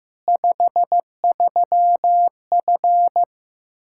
Morse code, heard as a keyed tone, about 15 wpm, 700 Hz. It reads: 53F
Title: Practice QSO - Exchange 2 - Info - 15wpm